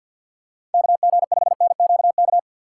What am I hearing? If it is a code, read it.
Morse code, heard as a keyed tone, about 52 wpm, 700 Hz. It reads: KG5NQK
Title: USA Call Signs - 6 Characters - 6x Character Spacing - 50wpm